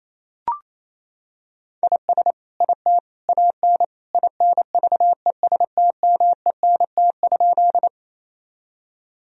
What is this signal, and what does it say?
Morse code, heard as a keyed tone, about 27 wpm, 700 Hz. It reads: IS IT AN INVESTMENT?
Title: Sentences from Top 1000 Words - Review Part 4 - 28wpm